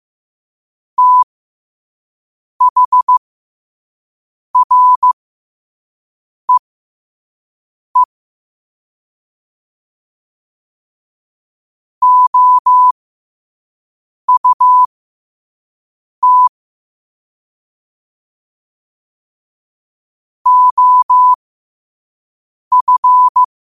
Morse code, heard as a keyed tone, about 15 wpm, spaced as for 5 wpm, 1000 Hz. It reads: THREE OUT OF
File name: qotdfiles / 5wpm